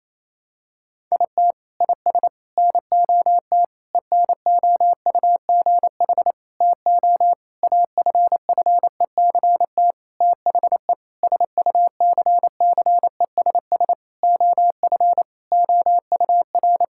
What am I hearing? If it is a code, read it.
Morse code, heard as a keyed tone, about 28 wpm, 700 Hz. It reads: IT IS NOT ENOUGH TO AFFECT THE SUCCESS OF OUR
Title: Sentences from Top 1000 Words - Review Part 6 - 28wpm